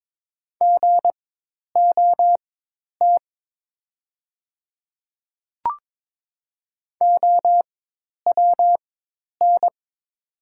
Morse code, heard as a keyed tone, about 22 wpm, spaced as for 10 wpm, 700 Hz. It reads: GOT OWN
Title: 4x Character Spacing - Top 300 Words - 22wpm